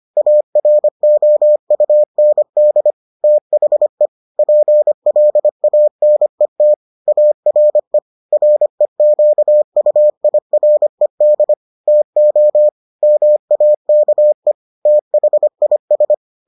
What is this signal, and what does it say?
Morse code, heard as a keyed tone, about 25 wpm, 600 Hz. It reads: AROUND THE PLANET ARE REQUIRED TO MAKE THIS